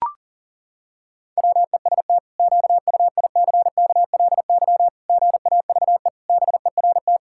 Morse code, heard as a keyed tone, about 40 wpm, 700 Hz. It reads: WEST QUICKLY GAVE BERT
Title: Pangram Challenge v2 - 40wpm